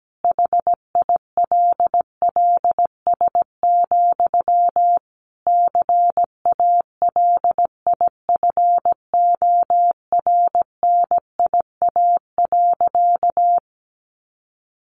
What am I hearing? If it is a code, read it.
Morse code, heard as a keyed tone, about 17 wpm, 700 Hz. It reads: HILLS, CALIFORNIA.
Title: Easy Phrases 17wpm